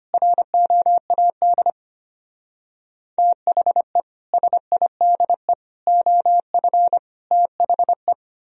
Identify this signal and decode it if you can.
Morse code, heard as a keyed tone, about 31 wpm, 700 Hz. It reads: ROAD THE SIDE OF THE
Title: Speed-Racing - Sets of 5 Words 25wpm